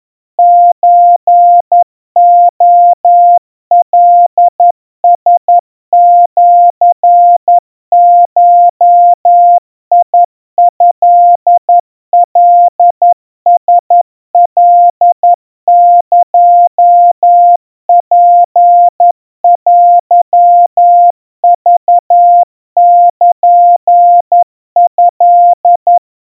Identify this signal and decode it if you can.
Morse code (Japanese Wabun code), heard as a keyed tone, about 11 wpm, 700 Hz. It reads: ソレカラシゴトカラカエツテクルト
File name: kawa / Morse sample / kamyu 40 cpm 700 Hz Crispy